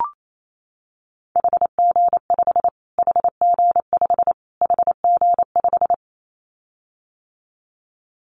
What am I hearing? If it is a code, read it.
Morse code, heard as a keyed tone, about 28 wpm, 700 Hz. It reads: HG5 HG5 HG5
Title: Callsign Prefixes for Top 25 DXCC - Repeated 3x - 28wpm